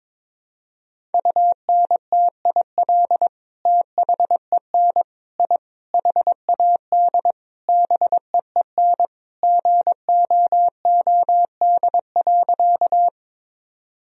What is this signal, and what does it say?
Morse code, heard as a keyed tone, about 22 wpm, 700 Hz. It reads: UNTIL THEN I HAD BEEN GOOD.
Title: Sentences from Top 1000 Words - Review Part 1 - 22wpm